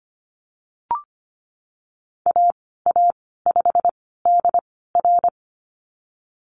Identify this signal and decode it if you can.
Morse code, heard as a keyed tone, about 25 wpm, spaced as for 16 wpm, 700 Hz. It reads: AA5DR
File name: Lesson 15g - Number 5 - Call Signs 25wpm